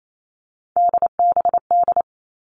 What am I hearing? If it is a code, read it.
Morse code, heard as a keyed tone, about 28 wpm, 700 Hz. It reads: DBD
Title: All Cut Numbers - 3-Digit 28wpm